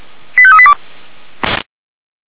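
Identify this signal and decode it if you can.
Five-tone selective call (ZVEI), heard as quick runs of five tones, tones of about 70 ms. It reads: 85382
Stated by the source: ZVEI five-tone selective call 85751 received with an SDR, logged by the Orlamünde fire brigade (feuerwehr-orlamuende).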